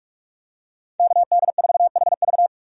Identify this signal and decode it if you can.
Morse code, heard as a keyed tone, about 46 wpm, 700 Hz. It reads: KD4HV
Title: USA Call Signs - 5 Characters - 45wpm